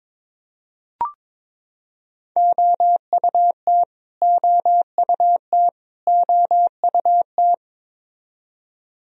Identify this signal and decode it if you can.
Morse code, heard as a keyed tone, about 22 wpm, 700 Hz. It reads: OUT OUT OUT
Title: Top 100 Words Repeated 22wpm